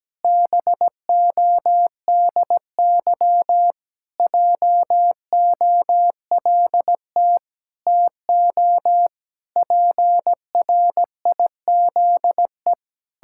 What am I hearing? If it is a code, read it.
Morse code, heard as a keyed tone, about 17 wpm, 700 Hz. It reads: BODY JOLT TO PRIZE
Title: Pangram Challenge v2 - 17wpm